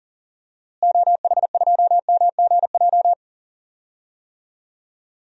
Morse code, heard as a keyed tone, about 40 wpm, 700 Hz. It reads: OH2MGJ
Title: DXCC-100 Call Signs - 6 Characters - Repeated 3x - 40wpm